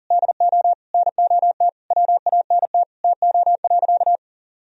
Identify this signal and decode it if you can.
Morse code, heard as a keyed tone, about 37 wpm, 700 Hz. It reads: DO NOT WANT TO.
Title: Sentences from Top 1000 Words - Review Part 3 - 40wpm